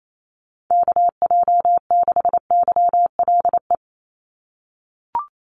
Morse code, heard as a keyed tone, about 28 wpm, 700 Hz. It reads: KJ6YLE
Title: USA Call Signs 28wpm